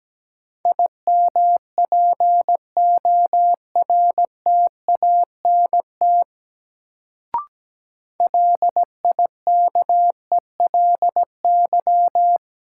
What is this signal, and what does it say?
Morse code, heard as a keyed tone, about 17 wpm, 700 Hz. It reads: IMPORTANT LIKELY
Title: Sentences from Top 600 Words - New Word Drill - 17wpm